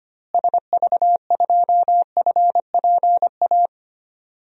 Morse code, heard as a keyed tone, about 25 wpm, 700 Hz. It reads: SV2FPA
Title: DXCC-100 Call Signs - 6 Characters - Repeated 3x - 25wpm